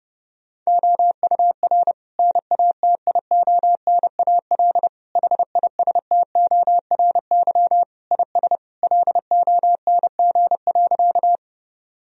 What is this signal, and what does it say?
Morse code, heard as a keyed tone, about 32 wpm, 700 Hz. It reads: OUR NATIONAL HISTORY IS LONG.
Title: Sentences from Top 1000 Words - Review Part 4 - 30wpm